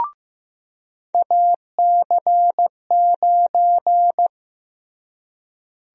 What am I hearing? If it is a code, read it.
Morse code, heard as a keyed tone, about 15 wpm, 700 Hz. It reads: AC9
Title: USA Call Sign Prefixes - Repeated 2x - 15wpm